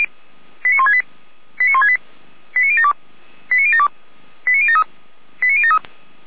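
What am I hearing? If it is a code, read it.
Five-tone selective call (ZVEI), heard as quick runs of five tones, tones of about 70 ms. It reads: 89168 89168 89072 89072 89073 89073
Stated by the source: ZVEI five-tone selective call 89150 received by SDR.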